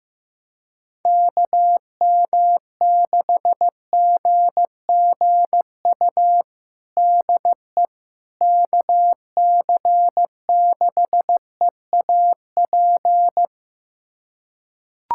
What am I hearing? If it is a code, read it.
Morse code, heard as a keyed tone, about 15 wpm, 700 Hz. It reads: KM6GGU DE KC6EAP
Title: USA Callsign DE Callsign 15wpm